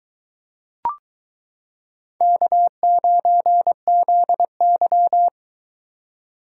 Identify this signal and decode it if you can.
Morse code, heard as a keyed tone, about 22 wpm, 700 Hz. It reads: K9ZY